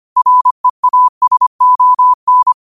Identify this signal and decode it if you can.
Morse code, heard as a keyed tone, about 25 wpm, 1000 Hz. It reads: REASON